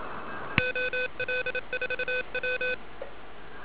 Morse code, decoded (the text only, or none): OL4W